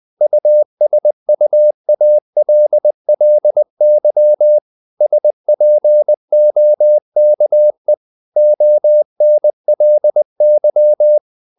Morse code, heard as a keyed tone, about 20 wpm, 600 Hz. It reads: USUALLY SPOKE ONLY
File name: the old man and the sea / the old man and the sea part 1 20wpm